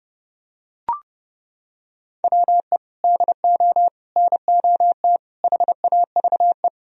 Morse code, heard as a keyed tone, about 30 wpm, 700 Hz. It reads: WE DO NOT HAVE